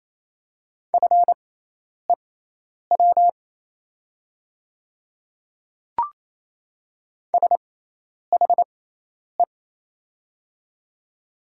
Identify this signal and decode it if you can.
Morse code, heard as a keyed tone, about 27 wpm, spaced as for 10 wpm, 690 Hz. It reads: FEW SHE
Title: 6x Character Spacing - Top 300 Words - 28wpm